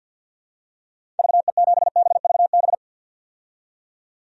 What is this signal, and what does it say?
Morse code, heard as a keyed tone, about 51 wpm, 700 Hz. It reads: VE7BVB
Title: DXCC-100 Call Signs - 6 Characters - 50wpm